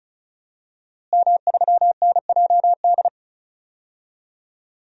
Morse code, heard as a keyed tone, about 36 wpm, 700 Hz. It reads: M3NJD
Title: DXCC-100 Call Signs - 5 Characters - Repeated 2x - 35wpm